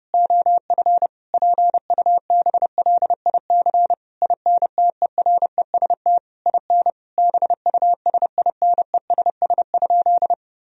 Morse code, heard as a keyed tone, about 30 wpm, 700 Hz. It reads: OF PUBLIC INTEREST IN BUSINESS?
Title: Sentences from Top 1000 Words - Review Part 3 - 30wpm